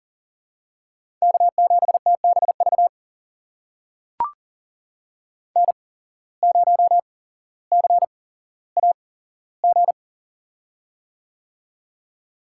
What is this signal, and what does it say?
Morse code, heard as a keyed tone, about 37 wpm, spaced as for 17 wpm, 700 Hz. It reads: K7TBV N0CAG
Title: USA Call Signs - 5 Characters - 8x Character Spacing - 40wpm